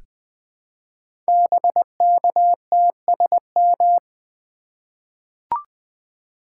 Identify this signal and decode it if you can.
Morse code, heard as a keyed tone, about 20 wpm, 700 Hz. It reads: BKTSM
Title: Code Groups - 5 Letters - 20wpm